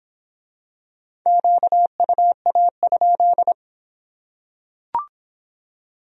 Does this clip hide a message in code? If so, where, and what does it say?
Yes: Morse code from 1.3 to 3.5 s, heard as a keyed tone, about 23 wpm, 700 Hz. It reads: QUA?